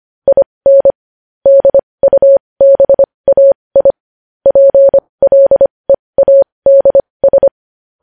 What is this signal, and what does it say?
Morse code, heard as a keyed tone, about 25 wpm, spaced as for 20 wpm, 550 Hz. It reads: IN DUBAI PLEADS
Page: news headlines in morse code at 20 wpm